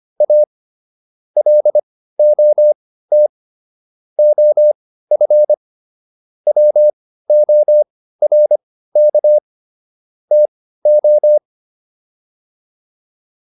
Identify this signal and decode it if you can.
Morse code, heard as a keyed tone, about 25 wpm, spaced as for 15 wpm, 600 Hz. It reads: A LOT OF WORK TO
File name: WD404 15